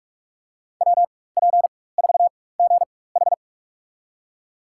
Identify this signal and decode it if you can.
Morse code, heard as a keyed tone, about 43 wpm, spaced as for 21 wpm, 700 Hz. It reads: WP4GH
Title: USA Call Signs - 5 Characters - 4x Character Spacing - 45wpm